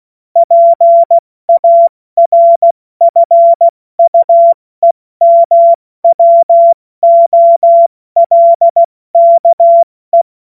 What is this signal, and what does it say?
Morse code, heard as a keyed tone, about 16 wpm, 680 Hz. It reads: PARFUEMWOLKE
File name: Morsekurs / Witze / Witz 31. August 2021, 70 F80